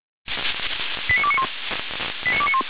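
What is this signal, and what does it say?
Five-tone selective call (ZVEI), heard as quick runs of five tones, tones of about 70 ms. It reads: 89291
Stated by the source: ZVEI five-tone selective call received by SDR.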